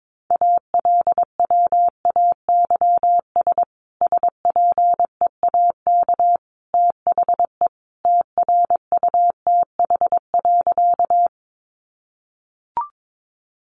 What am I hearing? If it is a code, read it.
Morse code, heard as a keyed tone, about 22 wpm, 700 Hz. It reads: ALWAYS SPEAK THE TRUTH.
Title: Sentences from Top 1100 Words 22wpm